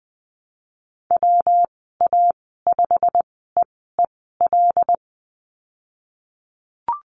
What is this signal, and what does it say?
Morse code, heard as a keyed tone, about 20 wpm, spaced as for 15 wpm, 700 Hz. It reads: WA5EEL
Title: 161 Lesson 26g - Letter g - Call Signs 20wpm